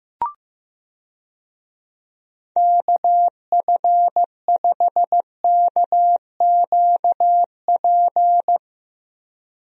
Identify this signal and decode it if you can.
Morse code, heard as a keyed tone, about 16 wpm, 700 Hz. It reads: KF5KQP
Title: USA Call Signs - 6 Characters - 15wpm